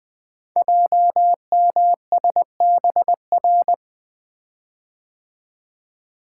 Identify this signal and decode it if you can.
Morse code, heard as a keyed tone, about 20 wpm, 700 Hz. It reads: JMSBR
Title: Code Groups - 5 Letters - 20wpm